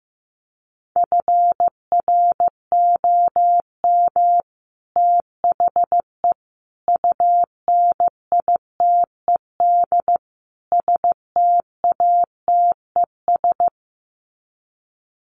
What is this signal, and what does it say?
Morse code, heard as a keyed tone, about 15 wpm, 700 Hz. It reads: FROM THE UNITED STATES
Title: Sets of 4 Words 15wpm